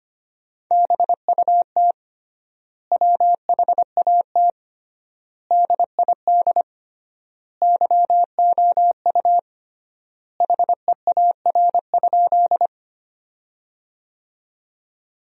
Morse code, heard as a keyed tone, about 25 wpm, 700 Hz. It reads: BUT WHAT DID YOU HEAR?
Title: 3x Word-Spacing - Sentences from Top 300 Words - 25wpm